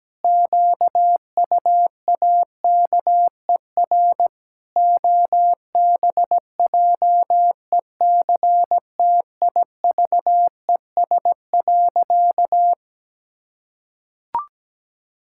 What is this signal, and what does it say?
Morse code, heard as a keyed tone, about 17 wpm, 700 Hz. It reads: QUAKER OBJECTIVES.
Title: Pangram Challenge v2 - 17wpm